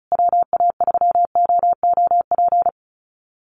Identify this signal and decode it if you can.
Morse code, heard as a keyed tone, about 34 wpm, 700 Hz. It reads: WA3OOP
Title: USA Call Signs Repeated 3x 35wpm